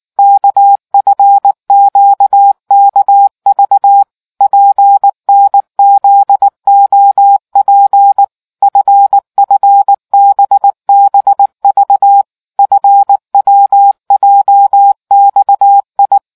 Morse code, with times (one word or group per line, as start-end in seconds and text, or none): 0.2-4.0 KFQKV
4.4-8.2 PNZOP
8.6-12.2 FFBBV
12.6-16.2 FWJXI